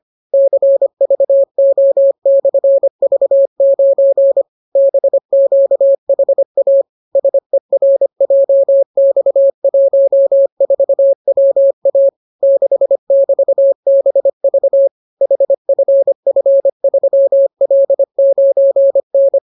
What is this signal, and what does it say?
Morse code, heard as a keyed tone, about 25 wpm, 550 Hz. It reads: CVO/V9 BQHA SERJX14WA 6<BT>BV HFF3L9N